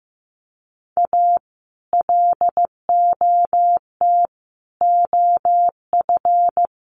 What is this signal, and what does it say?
Morse code, heard as a keyed tone, about 15 wpm, 700 Hz. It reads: A LOT OF
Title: Sets of 5 Words Encore2 15wpm